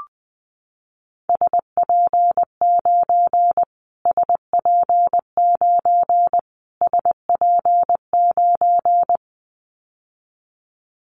Morse code, heard as a keyed tone, about 19 wpm, 700 Hz. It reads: SP9 SP9 SP9
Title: Callsign Prefixes for Top 25 DXCC - Repeated 3x - 20wpm